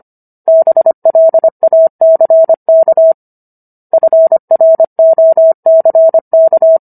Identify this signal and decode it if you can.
Morse code, heard as a keyed tone, about 25 wpm, 650 Hz. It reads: BLACK FROCK